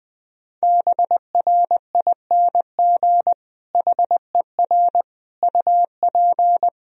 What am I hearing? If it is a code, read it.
Morse code, heard as a keyed tone, about 19 wpm, 700 Hz. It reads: BRING HER UP
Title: Sentences from Top 1000 Words - Review Part 3 - 20wpm